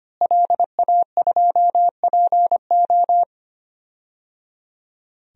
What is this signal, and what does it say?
Morse code, heard as a keyed tone, about 25 wpm, 700 Hz. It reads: LA2PO
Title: DXCC-100 Call Signs - 5 Characters - Repeated 3x - 25wpm